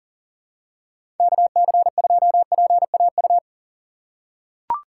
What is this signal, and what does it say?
Morse code, heard as a keyed tone, about 44 wpm, 700 Hz. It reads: KC2PAU
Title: USA Call Signs - 6 Characters - 6x Character Spacing - 40wpm